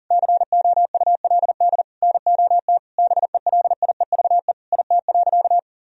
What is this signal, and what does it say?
Morse code, heard as a keyed tone, about 40 wpm, 700 Hz. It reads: COULD NOT BELIEVE IT.
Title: Sentences from Top 1000 Words - Review Part 3 - 40wpm